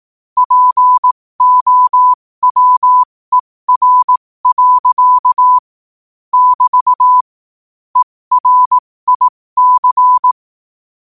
Morse code, heard as a keyed tone, about 18 wpm, spaced as for 15 wpm, 1000 Hz. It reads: POWER. <BT> ERIC
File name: qotdfiles / 15wpm